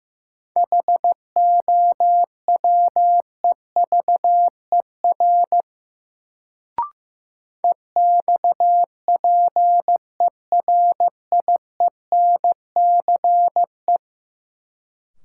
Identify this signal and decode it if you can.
Morse code, heard as a keyed tone, about 15 wpm, 700 Hz. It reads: HOWEVER EXPERIENCE
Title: Sentences from Top 600 Words - New Word Drill - 15wpm